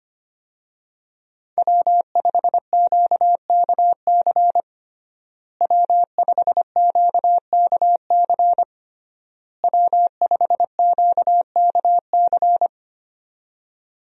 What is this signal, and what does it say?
Morse code, heard as a keyed tone, about 25 wpm, 700 Hz. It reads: W5QKC W5QKC W5QKC
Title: USA Call Signs - 5 Characters - Repeated 3x - 25wpm